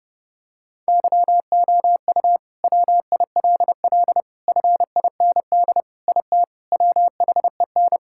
Morse code, heard as a keyed tone, about 31 wpm, 700 Hz. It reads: YOU WILL FIND IT WHEN
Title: Sentences from Top 1600 Words 30wpm